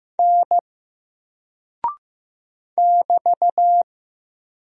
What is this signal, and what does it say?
Morse code, heard as a keyed tone, about 15 wpm, 700 Hz. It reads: N <BT>
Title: Single Character (Letter:Number:Punctuation:Prosign) Alt 15wpm